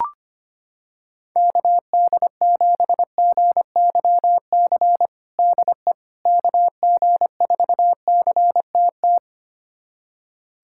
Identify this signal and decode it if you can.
Morse code, heard as a keyed tone, about 25 wpm, 700 Hz. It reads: KD7GYC DE KG4CTT
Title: USA Callsign DE Callsign - 6 Characters - 25wpm